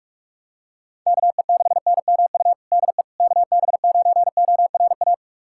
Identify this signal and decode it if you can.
Morse code, heard as a keyed tone, about 45 wpm, 700 Hz. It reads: KE6NMU DE KB9ORA
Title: USA Callsign DE Callsign 45wpm